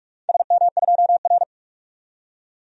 Morse code, heard as a keyed tone, about 43 wpm, 700 Hz. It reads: SM2R
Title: DXCC-100 Call Signs - 4 Characters - 45wpm